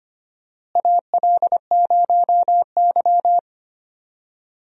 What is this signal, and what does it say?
Morse code, heard as a keyed tone, about 25 wpm, 700 Hz. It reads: AL0Y
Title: USA Call Signs - 4 Characters - Repeated 3x - 25wpm